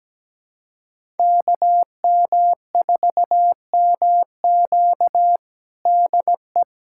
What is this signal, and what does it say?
Morse code, heard as a keyed tone, about 17 wpm, 700 Hz. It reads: KM4MQ DE
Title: USA Callsign DE Callsign - 5 Characters - 17wpm